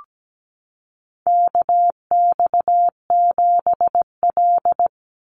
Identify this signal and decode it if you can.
Morse code, heard as a keyed tone, about 17 wpm, 700 Hz. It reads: KX7L
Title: NA SOTA Chaser Callsigns 17wpm